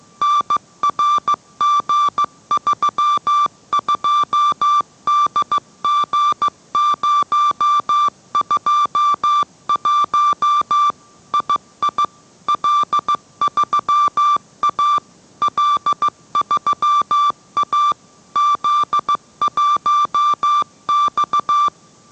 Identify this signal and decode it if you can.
Morse code, heard as a keyed tone, about 15 wpm, 1170 Hz. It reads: NRG32DG021 II L3A L3A Z1X